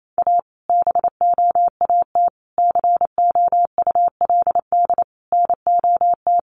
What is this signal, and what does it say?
Morse code, heard as a keyed tone, about 28 wpm, 700 Hz. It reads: A BOAT COULD NOT
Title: Sentences from Top 1400 Words 28wpm